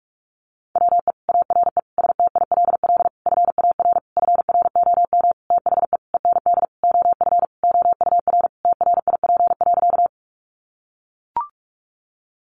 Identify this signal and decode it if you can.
Morse code, heard as a keyed tone, about 44 wpm, 700 Hz. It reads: WE ARE STILL FAR FROM THE END OF OUR TRIP.
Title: Sentences from Top 1300 Words 45wpm